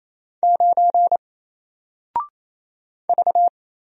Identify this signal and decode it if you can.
Morse code, heard as a keyed tone, about 28 wpm, 700 Hz. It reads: 9 V
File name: Single-Letter-Number-28wpm